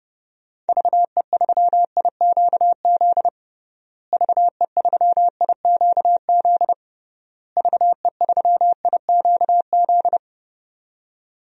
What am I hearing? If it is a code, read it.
Morse code, heard as a keyed tone, about 32 wpm, 700 Hz. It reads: VE3IQZ VE3IQZ VE3IQZ
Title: DXCC-100 Call Signs - 6 Characters - Repeated 3x - 30wpm